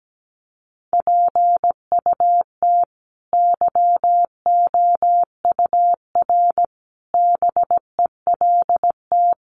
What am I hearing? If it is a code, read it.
Morse code, heard as a keyed tone, about 17 wpm, 700 Hz. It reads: PUT YOUR BELT
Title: Sentences from Top 1400 Words 17wpm